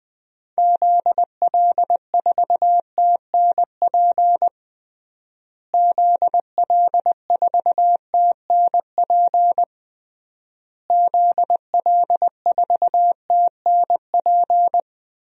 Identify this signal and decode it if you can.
Morse code, heard as a keyed tone, about 21 wpm, 700 Hz. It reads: ZL4TNP ZL4TNP ZL4TNP
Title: DXCC-100 Call Signs - 6 Characters - Repeated 3x - 20wpm